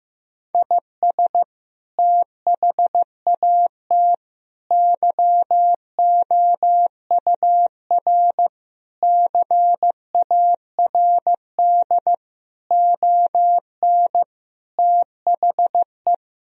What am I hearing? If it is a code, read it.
Morse code, heard as a keyed tone, about 15 wpm, 700 Hz. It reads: IS THAT YOUR CARD ON THE
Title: Sentences from Top 1000 Words - Review Part 4 - 15wpm